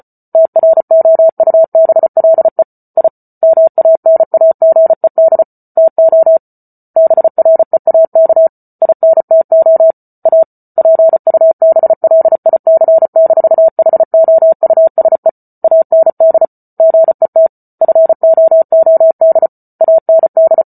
Morse code, heard as a keyed tone, about 34 wpm, 650 Hz. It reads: TROUBLE I MANAGED TO BREAK INTO A PUBLIC-HOUSE AND GET FOOD AND